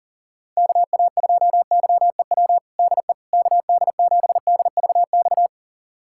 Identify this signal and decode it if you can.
Morse code, heard as a keyed tone, about 40 wpm, 700 Hz. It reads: KA2YEW DE KD7DVX
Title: USA Callsign DE Callsign - 6 Characters - 40wpm